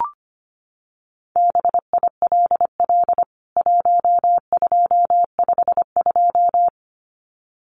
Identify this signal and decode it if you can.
Morse code, heard as a keyed tone, about 25 wpm, 700 Hz. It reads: BILL 1252
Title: CWT Exchange 25wpm